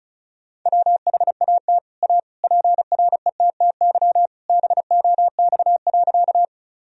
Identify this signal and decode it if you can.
Morse code, heard as a keyed tone, about 34 wpm, 700 Hz. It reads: WHAT A PRETTY BOX.
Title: Sentences from Top 1000 Words - Review Part 2 - 35wpm